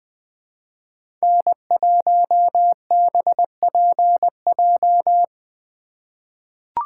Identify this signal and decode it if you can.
Morse code, heard as a keyed tone, about 20 wpm, 700 Hz. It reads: N1BPJ
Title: USA Call Signs - 3x Character Spacing - 20wpm